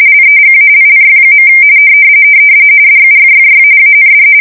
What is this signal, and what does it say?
RTTY, about 45.45 baud. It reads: LEGALLY. DUFFY AND
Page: 170 Shift 2125/2295